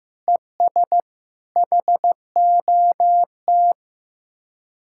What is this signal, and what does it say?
Morse code, heard as a keyed tone, about 15 wpm, 700 Hz. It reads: ES HOT
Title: Speed-Racing Common QSO Phrases Encore - 15wpm